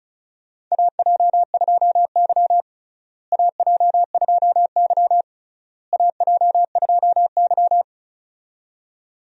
Morse code, heard as a keyed tone, about 36 wpm, 700 Hz. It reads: AJ2Y AJ2Y AJ2Y
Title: USA Call Signs - 4 Characters - Repeated 3x - 35wpm